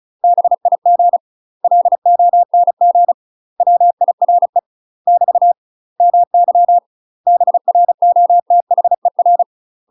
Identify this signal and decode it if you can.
Morse code, heard as a keyed tone, about 36 wpm, 700 Hz. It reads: BIG LONG WIRE <BT> MY BROTHER
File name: UK MASTER QSOs TRAINING / 35WPM / UK MASTER QSOs 0035